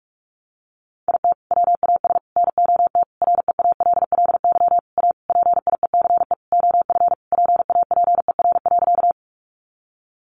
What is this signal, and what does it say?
Morse code, heard as a keyed tone, about 44 wpm, 700 Hz. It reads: IT WAS NOT REALLY A PIECE OF PAPER.
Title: Sentences from Top 600 Words 45wpm